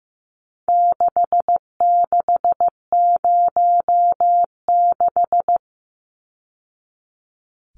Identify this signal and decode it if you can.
Morse code, heard as a keyed tone, about 15 wpm, 700 Hz. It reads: 6606
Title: Numbers - 4-Digit 15wpm